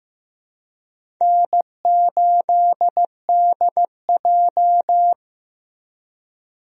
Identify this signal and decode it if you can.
Morse code, heard as a keyed tone, about 15 wpm, 700 Hz. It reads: N8DJ
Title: USA Call Signs - 4 Characters - Repeated 2x - 15wpm